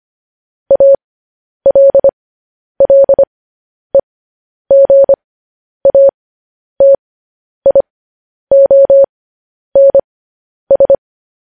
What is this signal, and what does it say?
Morse code, heard as a keyed tone, about 25 wpm, spaced as for 10 wpm, 550 Hz. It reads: ALLEGATIONS
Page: News Headlines in Morse Code at 10 WPM